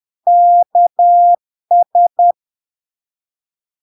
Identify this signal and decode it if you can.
Morse code, heard as a keyed tone, about 10 wpm, 700 Hz. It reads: KS
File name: LARK-FD-2022-Log-in-CW-10wpm-700-Hz